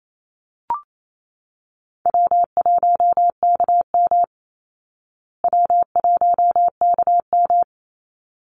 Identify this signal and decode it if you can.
Morse code, heard as a keyed tone, about 28 wpm, 700 Hz. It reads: W1KM W1KM
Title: Call Signs Repeated 3x - 28wpm